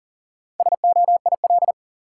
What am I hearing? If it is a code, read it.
Morse code, heard as a keyed tone, about 37 wpm, 700 Hz. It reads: SOIL
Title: Sentences from Top 1000 Words - Review Part 4 - New Word Drill - 40wpm